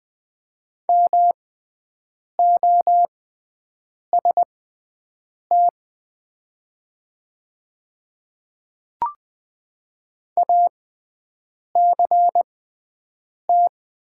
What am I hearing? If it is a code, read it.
Morse code, heard as a keyed tone, about 20 wpm, spaced as for 7 wpm, 700 Hz. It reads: MOST ACT
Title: 6x Character Spacing - Top 500 Words - 20wpm